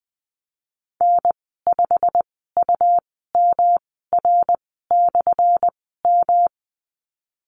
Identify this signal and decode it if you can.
Morse code, heard as a keyed tone, about 20 wpm, spaced as for 15 wpm, 700 Hz. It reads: N5UMR/M
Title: 179 Lesson 29g - Slash - Call Signs 20wpm